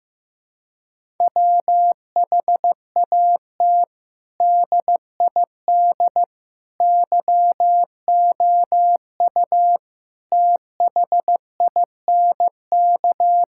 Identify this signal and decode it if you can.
Morse code, heard as a keyed tone, about 16 wpm, 700 Hz. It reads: WHAT DID YOU THINK
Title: Sentences from Top 1000 Words - Review Part 5 - 15wpm